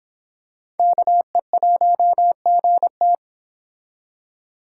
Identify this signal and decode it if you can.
Morse code, heard as a keyed tone, about 26 wpm, 700 Hz. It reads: KE1GT